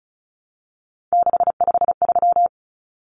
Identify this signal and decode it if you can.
Morse code, heard as a keyed tone, about 34 wpm, 700 Hz. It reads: 653
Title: Numbers - 3-Digit 35wpm